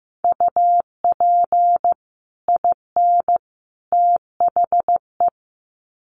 Morse code, heard as a keyed tone, about 15 wpm, 700 Hz. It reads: UP IN THE
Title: Sets of 3 Words 15wpm